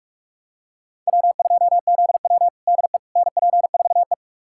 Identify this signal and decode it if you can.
Morse code, heard as a keyed tone, about 45 wpm, 700 Hz. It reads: W2ZW DE NP4E
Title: USA Callsign DE Callsign - 4 Characters - 45wpm